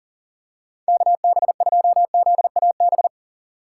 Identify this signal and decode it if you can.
Morse code, heard as a keyed tone, about 37 wpm, 700 Hz. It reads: KB2ZAB
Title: USA Call Signs - 6x Character Spacing - 40wpm